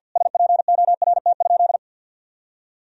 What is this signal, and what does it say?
Morse code, heard as a keyed tone, about 50 wpm, 700 Hz. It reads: SPORT?
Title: Sentences from Top 1000 Words - Review Part 4 - 50wpm